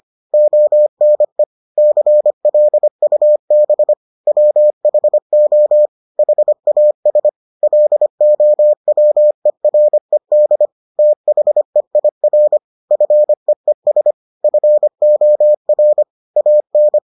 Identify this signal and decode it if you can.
Morse code, heard as a keyed tone, about 25 wpm, 600 Hz. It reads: ONE CLUB WHO HAS LOWERED THEIR FEES FOR AN